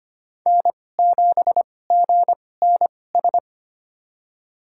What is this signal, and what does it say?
Morse code, heard as a keyed tone, about 25 wpm, spaced as for 18 wpm, 700 Hz. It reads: N7GNS
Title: USA Call Signs - 5 Characters - 2x Character Spacing - 25wpm